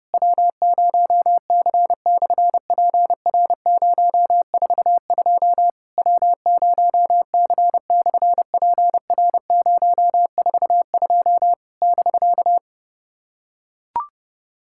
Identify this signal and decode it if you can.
Morse code, heard as a keyed tone, about 30 wpm, 700 Hz. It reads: W0C/PR042 W0C/PR042 <BK>
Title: SOTA QSO Practice 30wpm